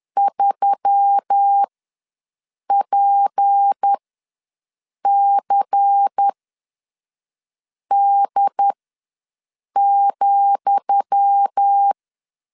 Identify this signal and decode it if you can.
Morse code, heard as a keyed tone, about 10 wpm, spaced as for 5 wpm, 800 Hz. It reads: 3PCD,
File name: LEZIONI / 10 SIGNS / 10D COMA ws 10wpm 4314